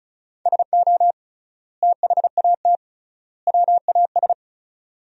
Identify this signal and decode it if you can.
Morse code, heard as a keyed tone, about 36 wpm, 700 Hz. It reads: SO THAT WAS
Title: 3x Word-Spacing - Sentences from Top 100 Words - 35wpm